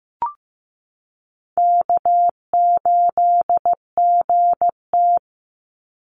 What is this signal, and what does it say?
Morse code, heard as a keyed tone, about 15 wpm, 700 Hz. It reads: K8GT
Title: Easy Call Signs Repeated 3x - 15wpm